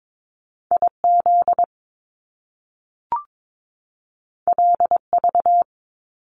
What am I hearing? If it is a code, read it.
Morse code, heard as a keyed tone, about 22 wpm, 700 Hz. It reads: IZ LV
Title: Top Two Letter Combinations in Words 22wpm